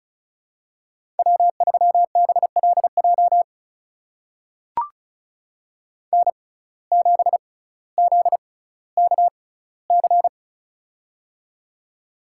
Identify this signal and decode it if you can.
Morse code, heard as a keyed tone, about 35 wpm, spaced as for 18 wpm, 700 Hz. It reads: W3BLJ N7ZKC
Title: USA Call Signs - 6x Character Spacing - 35wpm